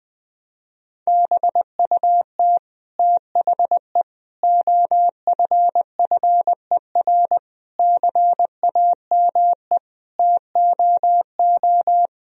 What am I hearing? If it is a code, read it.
Morse code, heard as a keyed tone, about 21 wpm, 700 Hz. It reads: BUT THE OFFER CAME TOO
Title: Sentences from Top 1000 Words - Review Part 4 - 20wpm